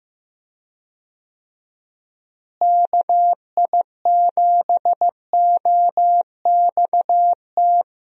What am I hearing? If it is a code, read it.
Morse code, heard as a keyed tone, about 15 wpm, 700 Hz. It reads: KI7OXT
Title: USA Call Signs - 6 Characters - Repeated 2x - 15wpm